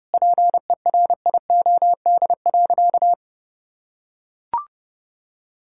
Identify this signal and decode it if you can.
Morse code, heard as a keyed tone, about 30 wpm, 700 Hz. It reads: PERIOD.
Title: Sentences from Top 1000 Words - Review Part 5 - 30wpm